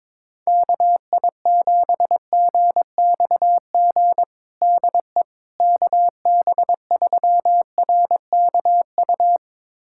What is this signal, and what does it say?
Morse code, heard as a keyed tone, about 22 wpm, 700 Hz. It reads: KI7GXG DE KB3RKU
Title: USA Callsign DE Callsign 22wpm